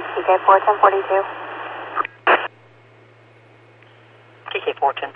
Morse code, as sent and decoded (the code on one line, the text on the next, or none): none